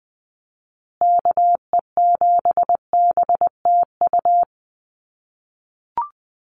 Morse code, heard as a keyed tone, about 20 wpm, 700 Hz. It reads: KE7BTU
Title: USA Call Signs Repeated 2x 20wpm